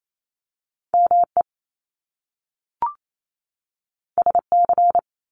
Morse code, heard as a keyed tone, about 28 wpm, 700 Hz. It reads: ME SC
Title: Top Two Letter Combinations in Words 28wpm